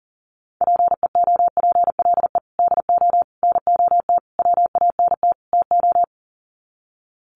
Morse code, heard as a keyed tone, about 40 wpm, 700 Hz. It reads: PEOPLE DO NOT WANT TO
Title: Sets of 5 Words Encore2 40wpm